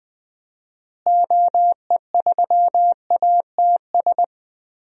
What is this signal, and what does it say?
Morse code, heard as a keyed tone, about 20 wpm, 700 Hz. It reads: OE3ATS
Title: DXCC-100 Call Signs - 6 Characters - Repeated 3x - 20wpm